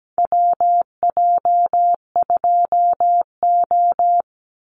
Morse code, heard as a keyed tone, about 17 wpm, 700 Hz. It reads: WJ2O
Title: Call Signs Repeated 3x - 17wpm